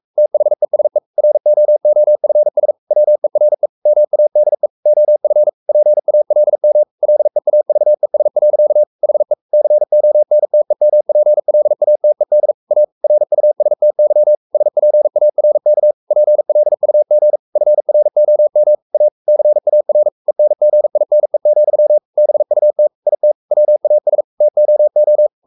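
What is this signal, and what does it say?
Morse code, heard as a keyed tone, about 44 wpm, 600 Hz. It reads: THESE ROOFS WERE MADE OF PALM LEAVES. HE CONTEMPLATED A RUSTY SPARK PLUG FROM A CAR ENGINE, BUT IT WAS TOO
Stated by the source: Morse code at 43 wpm sending Ken Follett's novel Never in English, from CW med Gnister.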